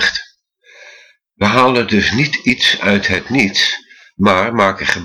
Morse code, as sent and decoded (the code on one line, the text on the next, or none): none